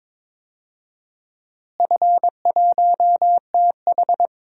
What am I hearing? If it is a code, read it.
Morse code, heard as a keyed tone, about 22 wpm, 700 Hz. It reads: F1TH
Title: DXCC-100 Call Signs - 4 Characters - Repeated 3x - 22wpm